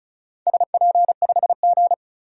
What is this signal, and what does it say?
Morse code, heard as a keyed tone, about 36 wpm, 690 Hz. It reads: SP5G
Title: DXCC-100 Call Signs - 4 Characters - 35wpm